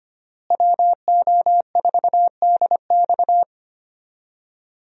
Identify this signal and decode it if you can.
Morse code, heard as a keyed tone, about 25 wpm, 700 Hz. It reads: WO4DX
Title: USA Call Signs - 5 Characters - Repeated 2x - 25wpm